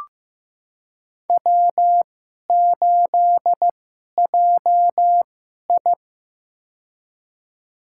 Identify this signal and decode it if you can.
Morse code, heard as a keyed tone, about 15 wpm, spaced as for 11 wpm, 700 Hz. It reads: W8JI
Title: USA Call Signs - 4 Characters - 2x Character Spacing - 15wpm